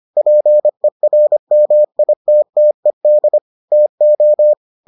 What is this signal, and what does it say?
Morse code, heard as a keyed tone, about 25 wpm, 600 Hz. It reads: PERMITTED TO